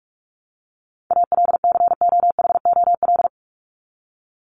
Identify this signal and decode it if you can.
Morse code, heard as a keyed tone, about 44 wpm, 700 Hz. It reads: ALCOHOL
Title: Top 1251-1500 Words 45wpm